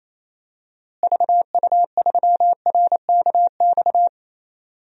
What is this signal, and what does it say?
Morse code, heard as a keyed tone, about 28 wpm, 700 Hz. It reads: VU3RKX
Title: DXCC-100 Call Signs - 6 Characters - Repeated 3x - 28wpm